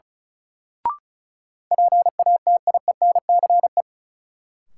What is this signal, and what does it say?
Morse code, heard as a keyed tone, about 36 wpm, 700 Hz. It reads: PATIENCE
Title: Sentences from Top 1300 Words - New Word Drill - 35wpm